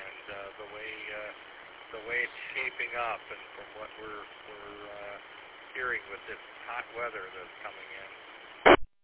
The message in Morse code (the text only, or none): none